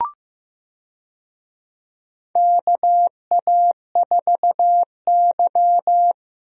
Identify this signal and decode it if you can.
Morse code, heard as a keyed tone, about 15 wpm, 700 Hz. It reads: KA4Y
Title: USA Call Signs - 4 Characters - Repeated 2x - 15wpm